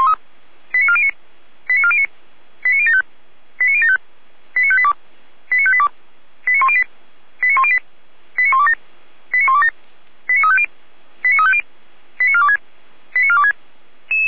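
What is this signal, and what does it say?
Five-tone selective call (ZVEI), heard as quick runs of five tones, tones of about 70 ms. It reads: 89409 89409 89075 89075 89582 89582 89108 89108 89127 89127 89350 89350 89426 89426
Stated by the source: ZVEI five-tone selective call received by SDR.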